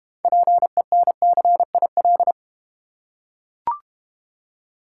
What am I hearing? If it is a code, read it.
Morse code, heard as a keyed tone, about 32 wpm, 700 Hz. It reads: PENCIL